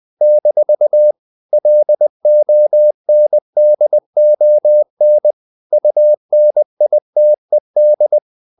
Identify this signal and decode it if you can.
Morse code, heard as a keyed tone, about 20 wpm, 600 Hz. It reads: - LONDON UNITED
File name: CW / Practice / NationalCapitals20